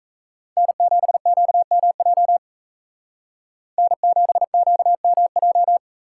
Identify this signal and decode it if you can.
Morse code, heard as a keyed tone, about 33 wpm, 700 Hz. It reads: N7QMJ N7QMJ
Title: Speed Racing - USA 1x3 Call Signs 28wpm